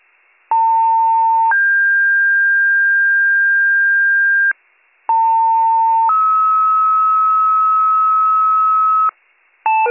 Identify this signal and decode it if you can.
Two-tone page, heard as a one-second tone, then a three-second tone, about 900 then 1600 Hz, about 900 then 1290 Hz, with noise. It is heard from 0.5 to 9.1 s.